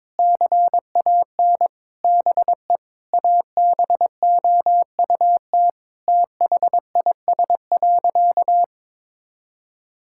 Morse code, heard as a keyed tone, about 22 wpm, 700 Hz. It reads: CAN BE ABOUT THIS.
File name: Sentences with 9-Word Phrases - 22wpm